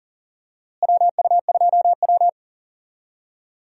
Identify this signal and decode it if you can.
Morse code, heard as a keyed tone, about 40 wpm, 700 Hz. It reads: WU2W
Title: USA Call Signs - 4 Characters - Repeated 3x - 40wpm